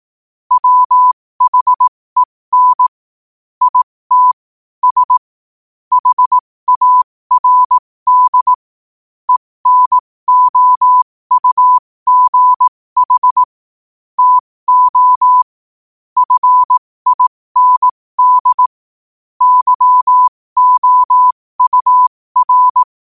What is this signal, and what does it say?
Morse code, heard as a keyed tone, about 18 wpm, spaced as for 15 wpm, 1000 Hz. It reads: WHEN IT S HARD ENOUGH TO FIND YOUR
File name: qotdfiles / 15wpm